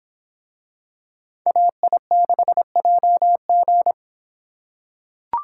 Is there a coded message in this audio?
Yes: Morse code from 1.5 to 3.9 s, heard as a keyed tone, about 25 wpm, 700 Hz. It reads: AI6JG